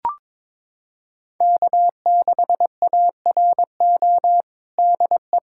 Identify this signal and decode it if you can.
Morse code, heard as a keyed tone, about 22 wpm, 700 Hz. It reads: K6ARO DE
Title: USA Callsign DE Callsign - 5 Characters - 22wpm